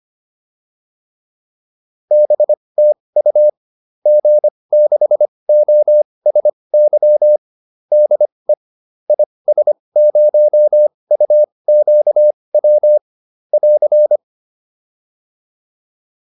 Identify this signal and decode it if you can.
Morse code, heard as a keyed tone, about 25 wpm, spaced as for 20 wpm, 600 Hz. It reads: BTU G6OSY DE IS0UQW <AR>